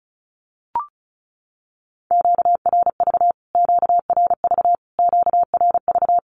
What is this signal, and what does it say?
Morse code, heard as a keyed tone, about 34 wpm, 700 Hz. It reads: QRV QRV QRV
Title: Q-Codes Repeated 3x 35wpm